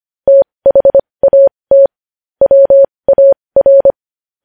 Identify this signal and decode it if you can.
Morse code, heard as a keyed tone, about 25 wpm, spaced as for 20 wpm, 550 Hz. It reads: THAT WAR
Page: news-headlines-in-morse-code-at-20-wpm